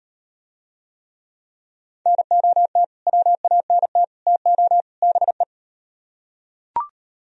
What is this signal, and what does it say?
Morse code, heard as a keyed tone, about 38 wpm, 690 Hz. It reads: NOT WANT TO BE